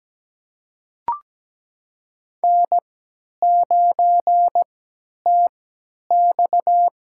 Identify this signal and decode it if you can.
Morse code, heard as a keyed tone, about 17 wpm, spaced as for 10 wpm, 700 Hz. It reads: N9TX
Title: USA Call Signs - 4 Characters - 3x Character Spacing - 17wpm